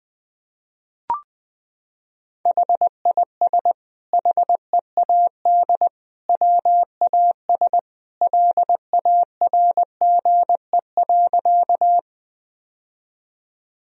Morse code, heard as a keyed tone, about 20 wpm, 700 Hz. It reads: HIS HEAD WAS LARGE.
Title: Sentences from Top 1000 Words - Review Part 1 - 20wpm